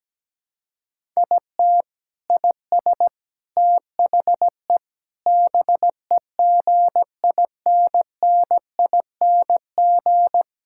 Morse code, heard as a keyed tone, about 17 wpm, 700 Hz. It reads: IT IS THE BEGINNING
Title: Sentences from Top 1000 Words - Review Part 5 - 17wpm